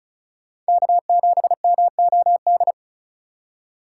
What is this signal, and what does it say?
Morse code, heard as a keyed tone, about 34 wpm, 700 Hz. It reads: K7MOD
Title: USA Call Signs - 5 Characters - 4x Character Spacing - 35wpm